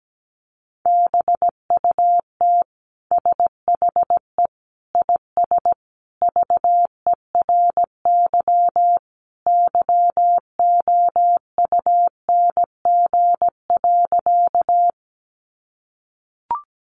Morse code, heard as a keyed tone, about 17 wpm, 700 Hz. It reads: BUT SHE IS VERY YOUNG.
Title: Sentences from Top 400 Words 17wpm